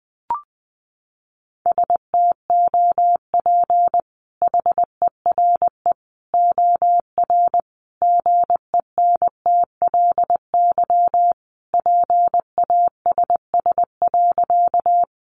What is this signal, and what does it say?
Morse code, heard as a keyed tone, about 20 wpm, 700 Hz. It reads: STOP HERE OR GENTLY PASS.
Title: Sentences from Top 1500 Words 20wpm